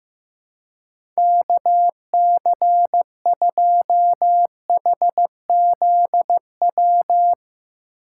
Morse code, heard as a keyed tone, about 15 wpm, 700 Hz. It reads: KC2HZW
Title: USA Call Signs - 6 Characters - 2x Character Spacing - 15wpm